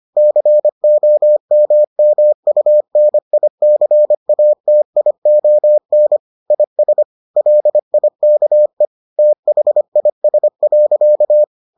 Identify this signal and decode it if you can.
Morse code, heard as a keyed tone, about 25 wpm, 600 Hz. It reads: COMMUNICATION IS LIKE THIS.